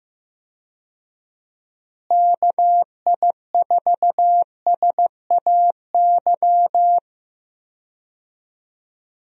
Morse code, heard as a keyed tone, about 15 wpm, 700 Hz. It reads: KI4SAY
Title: USA Call Signs - 6 Characters - Repeated 2x - 15wpm